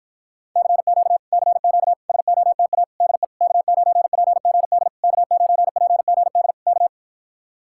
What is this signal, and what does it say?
Morse code, heard as a keyed tone, about 54 wpm, 700 Hz. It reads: CQ CQ SOTA DE K9PGD K9PGD K